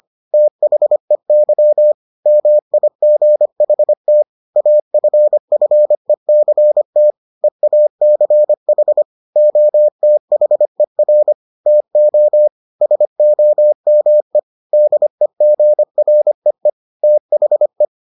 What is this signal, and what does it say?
Morse code, heard as a keyed tone, about 25 wpm, 600 Hz. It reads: THEY MIGHT AFFECT EACH OTHER TO SOME DEGREE THE